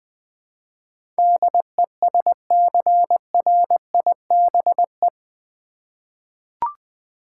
Morse code, heard as a keyed tone, about 20 wpm, 700 Hz. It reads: DESCRIBE
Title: Eight Letter Words - 20wpm